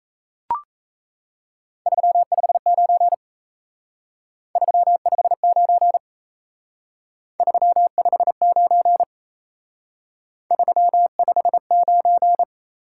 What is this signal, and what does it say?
Morse code, heard as a keyed tone, about 35 wpm, 700 Hz. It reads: 359 359 359 359
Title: Speed-Racing Top 180 QSO Elements 28wpm